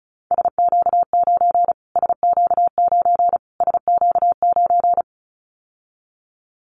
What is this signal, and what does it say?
Morse code, heard as a keyed tone, about 34 wpm, 700 Hz. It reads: SQ9 SQ9 SQ9
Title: Callsign Prefixes for Top 25 DXCC - Repeated 3x - 35wpm